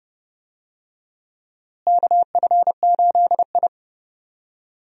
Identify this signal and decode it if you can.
Morse code, heard as a keyed tone, about 30 wpm, 700 Hz. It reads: KF8I
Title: USA Call Signs - 4 Characters - Repeated 2x - 30wpm